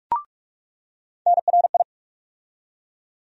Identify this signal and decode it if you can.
Morse code, heard as a keyed tone, about 46 wpm, 700 Hz. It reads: NRI
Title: USA Call Sign Postfixes - 3 Letters 45wpm